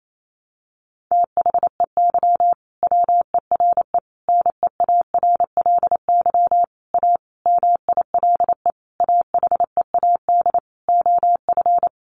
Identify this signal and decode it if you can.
Morse code, heard as a keyed tone, about 28 wpm, 700 Hz. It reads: THEY WERE NEARLY A MILE AHEAD OF
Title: Sentences from Top 1300 Words 28wpm